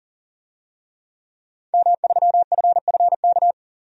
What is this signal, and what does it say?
Morse code, heard as a keyed tone, about 40 wpm, 700 Hz. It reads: M3FFK
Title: DXCC-100 Call Signs - 5 Characters - 40wpm